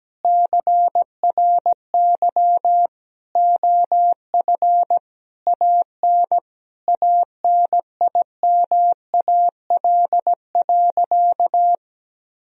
Morse code, heard as a keyed tone, about 17 wpm, 700 Hz. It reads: CRY OF AN ANIMAL.
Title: Sentences from Top 1000 Words - Review Part 2 - 17wpm